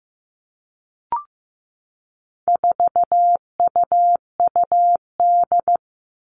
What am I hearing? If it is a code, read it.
Morse code, heard as a keyed tone, about 15 wpm, 700 Hz. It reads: 4UUD
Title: All Cut Numbers - 4-Digit 15wpm